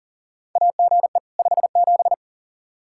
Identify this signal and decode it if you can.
Morse code, heard as a keyed tone, about 40 wpm, 700 Hz. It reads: AGE 57
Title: Practice QSO - Exchange 3 - More Info - 40wpm